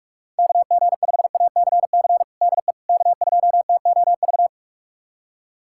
Morse code, heard as a keyed tone, about 46 wpm, 690 Hz. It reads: KG5ACC DE K2TOV
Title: USA Callsign DE Callsign 45wpm